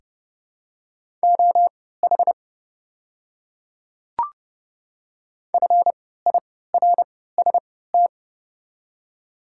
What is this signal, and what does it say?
Morse code, heard as a keyed tone, about 31 wpm, spaced as for 17 wpm, 700 Hz. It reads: OH FIRST
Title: 3x Character Spacing - Top 500 Words - 30wpm